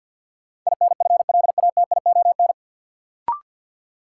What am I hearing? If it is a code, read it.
Morse code, heard as a keyed tone, about 53 wpm, 700 Hz. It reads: INFLATION
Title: Sentences from Top 1100 Words - New Word Drill - 50wpm